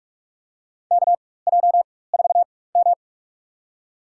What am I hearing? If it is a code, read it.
Morse code, heard as a keyed tone, about 46 wpm, spaced as for 21 wpm, 700 Hz. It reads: KJ4M
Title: USA Call Signs - 4 Characters - 4x Character Spacing - 45wpm